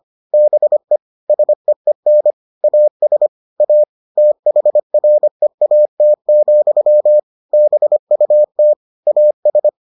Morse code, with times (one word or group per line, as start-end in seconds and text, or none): 0.3-1.0 BE
1.3-2.3 SEEN
2.6-3.3 AS
3.6-3.8 A
4.2-7.2 THREAT,
7.5-8.7 BUT
9.1-9.7 AS